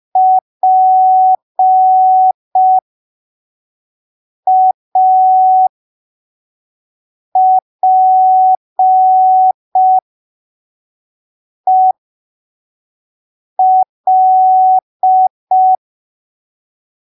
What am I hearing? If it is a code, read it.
Morse code, heard as a keyed tone, about 5 wpm, spaced as for 3 wpm, 750 Hz. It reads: PAPEL
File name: arquivos / texto / texto 02 5ppm